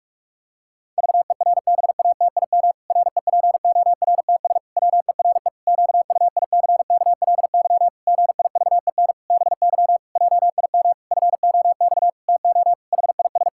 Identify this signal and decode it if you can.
Morse code, heard as a keyed tone, about 46 wpm, 700 Hz. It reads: VERBATIM REPORTS WERE QUICKLY GIVEN BY JIM FOX TO HIS
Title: Pangram Challenge v2 - 45wpm